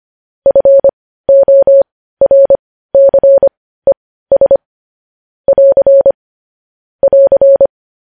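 Morse code, heard as a keyed tone, about 25 wpm, spaced as for 15 wpm, 550 Hz. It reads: FORCES <AR> <AR>
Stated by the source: Morse code transcription of news headlines.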